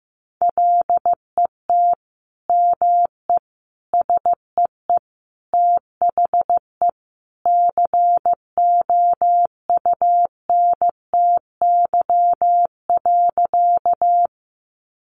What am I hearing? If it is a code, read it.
Morse code, heard as a keyed tone, about 15 wpm, 700 Hz. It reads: LET ME SEE THE COUNTY.
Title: Sentences from Top 1200 Words 15wpm